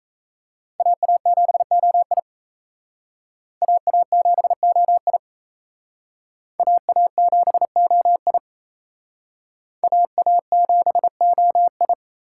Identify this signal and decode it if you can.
Morse code, heard as a keyed tone, about 35 wpm, 700 Hz. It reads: AA7OI AA7OI AA7OI AA7OI
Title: Speed Racing - USA 2x2 Call Signs 28wpm